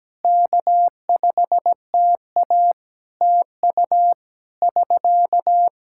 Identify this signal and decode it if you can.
Morse code, heard as a keyed tone, about 17 wpm, 700 Hz. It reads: K5TA TU <SK>
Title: Practice QSO - Exchange 4 - Ending - 17wpm